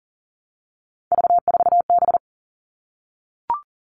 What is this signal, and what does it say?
Morse code, heard as a keyed tone, about 40 wpm, 700 Hz. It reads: V4B
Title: All Cut Numbers - 3-Digit 40wpm